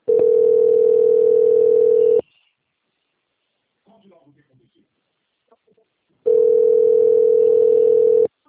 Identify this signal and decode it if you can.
Ringback tone (North America), 440 + 480 Hz, with speech.
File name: out- - - - - .87968